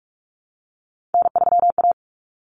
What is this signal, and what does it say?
Morse code, heard as a keyed tone, about 44 wpm, 700 Hz. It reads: N3A
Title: Common Cut Numbers - 3-Digit 45wpm